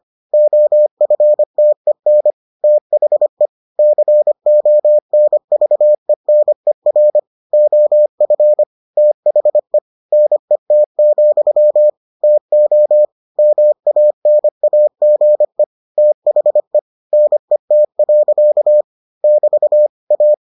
Morse code, heard as a keyed tone, about 25 wpm, 600 Hz. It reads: OFTEN THE CONVENER OF THE NET, TO MANAGE THE NET. <BT> A